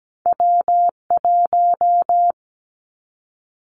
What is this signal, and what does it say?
Morse code, heard as a keyed tone, about 17 wpm, 700 Hz. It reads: W1
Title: Callsign Prefixes for Top 25 DXCC - 17wpm